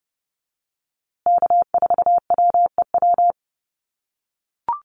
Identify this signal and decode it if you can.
Morse code, heard as a keyed tone, about 30 wpm, 700 Hz. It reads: K4WEW
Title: USA Call Signs Repeated 2x 30wpm